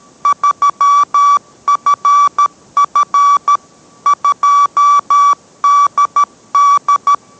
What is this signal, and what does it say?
Morse code, heard as a keyed tone, about 13 wpm, 1150 Hz. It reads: 3FF 2DD